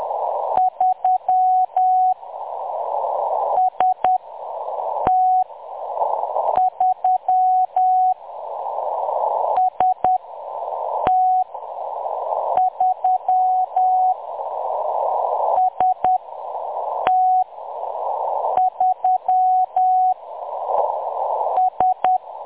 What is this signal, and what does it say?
Morse code, heard as a keyed tone, about 10 wpm, spaced as for 6 wpm, 740 Hz. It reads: 3 ST3 ST3 ST3 S
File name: M23-ST3- -0858utc-10184khz